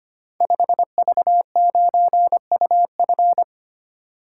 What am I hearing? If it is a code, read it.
Morse code, heard as a keyed tone, about 25 wpm, 700 Hz. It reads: 5V9UF
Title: Code Groups - 5 Letter:Numbers - 25wpm